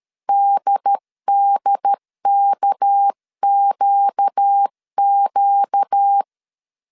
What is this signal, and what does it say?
Morse code, heard as a keyed tone, about 13 wpm, 800 Hz. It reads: DDKQQ
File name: CW / DET / 1 6 1500C S C 12wpm 3533